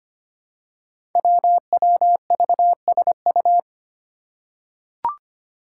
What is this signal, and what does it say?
Morse code, heard as a keyed tone, about 25 wpm, 700 Hz. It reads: WWVSU
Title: Five Challenging Letter-Number Combinations 25wpm